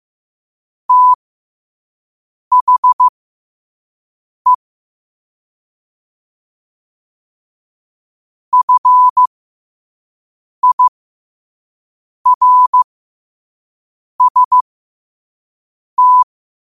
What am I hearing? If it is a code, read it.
Morse code, heard as a keyed tone, about 15 wpm, spaced as for 5 wpm, 1000 Hz. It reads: THE FIRST